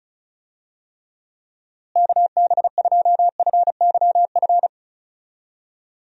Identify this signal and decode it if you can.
Morse code, heard as a keyed tone, about 36 wpm, 700 Hz. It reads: KB2FYF
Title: USA Call Signs - 6 Characters - Repeated 3x - 35wpm